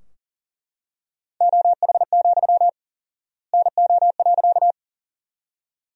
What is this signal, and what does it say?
Morse code, heard as a keyed tone, about 40 wpm, 700 Hz. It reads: OH, NO.
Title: 4x Word-Spacing - Sentences from Top 500 Words - 40wpm